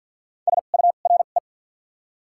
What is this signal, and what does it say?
Morse code, heard as a keyed tone, about 48 wpm, spaced as for 35 wpm, 700 Hz. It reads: SURE
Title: 2x Character Spacing - Top 500 Words - 50wpm